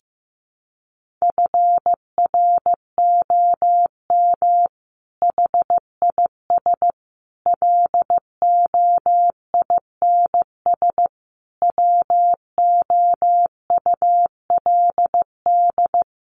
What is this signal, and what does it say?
Morse code, heard as a keyed tone, about 15 wpm, 700 Hz. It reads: FROM HIS LOINS WOULD
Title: A Sound Of Thunder 15wpm